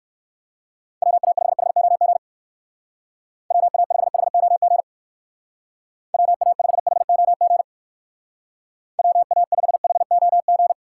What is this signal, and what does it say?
Morse code, heard as a keyed tone, about 59 wpm, 700 Hz. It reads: WA5HOG WA5HOG WA5HOG WA5HOG